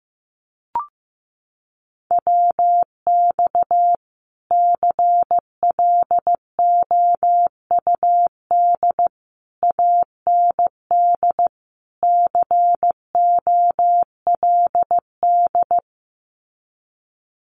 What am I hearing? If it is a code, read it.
Morse code, heard as a keyed tone, about 15 wpm, 700 Hz. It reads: WX CLOUD AND COLD
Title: Common QSO Phrases Encore 15wpm